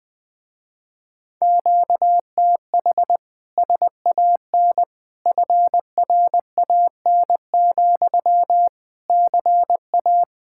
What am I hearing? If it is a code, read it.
Morse code, heard as a keyed tone, about 20 wpm, 700 Hz. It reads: QTH SAN FRAN, CA
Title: Speed-Racing Common QSO Phrases Encore - 17wpm